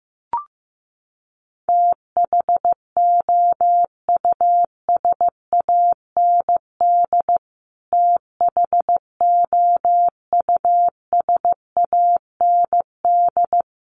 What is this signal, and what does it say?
Morse code, heard as a keyed tone, about 15 wpm, 700 Hz. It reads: THOUSAND THOUSAND
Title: Top 500 Words Repeated 15wpm